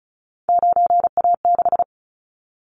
Morse code, heard as a keyed tone, about 34 wpm, 700 Hz. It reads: 9A6
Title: Callsign Prefixes for Top 100 DXCC - 35wpm